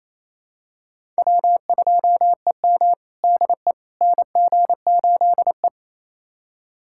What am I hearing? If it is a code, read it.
Morse code, heard as a keyed tone, about 28 wpm, 700 Hz. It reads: W2EM DE NG8E
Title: USA Callsign DE Callsign - 4 Characters - 28wpm